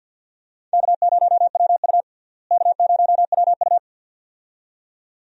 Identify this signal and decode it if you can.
Morse code, heard as a keyed tone, about 46 wpm, 700 Hz. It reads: K0WU K0WU
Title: USA Call Signs - 4 Characters - Repeated 2x - 50wpm